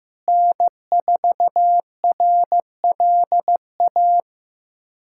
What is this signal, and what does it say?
Morse code, heard as a keyed tone, about 15 wpm, 700 Hz. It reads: N4RLA
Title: USA Callsign DE Callsign - 5 Characters - 15wpm